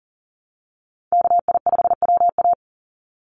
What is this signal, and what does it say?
Morse code, heard as a keyed tone, about 40 wpm, 700 Hz. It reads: KI5WA
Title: NA SOTA Chaser Callsigns 40wpm